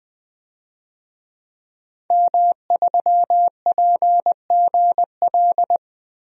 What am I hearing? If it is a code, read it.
Morse code, heard as a keyed tone, about 20 wpm, 700 Hz. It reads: M3PGL
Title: DXCC-100 Call Signs - 5 Characters - Repeated 2x - 20wpm